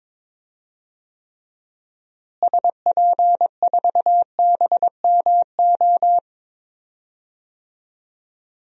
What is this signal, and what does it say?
Morse code, heard as a keyed tone, about 22 wpm, 700 Hz. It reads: SP4BMO
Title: DXCC-100 Call Signs - 6 Characters - Repeated 3x - 22wpm